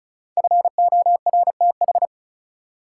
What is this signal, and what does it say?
Morse code, heard as a keyed tone, about 35 wpm, 700 Hz. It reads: FORTH
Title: Five Letter Words - 35wpm